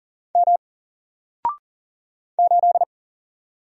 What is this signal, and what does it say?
Morse code, heard as a keyed tone, about 40 wpm, 700 Hz. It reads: M 8
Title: Single Letter-Number Alt 40wpm